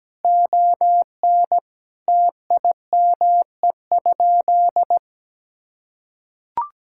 Morse code, heard as a keyed tone, about 17 wpm, 700 Hz. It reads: ON TIME?
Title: Sentences from Top 1000 Words - Review Part 5 - 17wpm